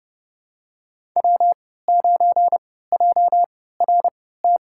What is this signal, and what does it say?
Morse code, heard as a keyed tone, about 30 wpm, spaced as for 17 wpm, 700 Hz. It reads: W9JRT
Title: USA Call Signs - 5 Characters - 3x Character Spacing - 30wpm